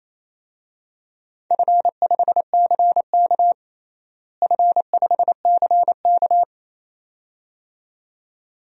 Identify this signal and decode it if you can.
Morse code, heard as a keyed tone, about 28 wpm, 700 Hz. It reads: F5CK F5CK
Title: DXCC-100 Call Signs - 4 Characters - Repeated 2x - 28wpm